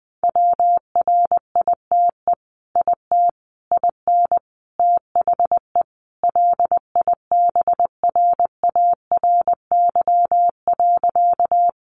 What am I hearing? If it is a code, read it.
Morse code, heard as a keyed tone, about 20 wpm, 700 Hz. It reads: WRITE IT IN THE LIBRARY.
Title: Sentences from Top 600 Words 20wpm